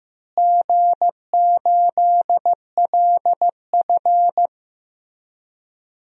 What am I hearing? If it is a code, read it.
Morse code, heard as a keyed tone, about 15 wpm, 700 Hz. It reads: G8LF
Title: DXCC-100 Call Signs - 4 Characters - Repeated 3x - 15wpm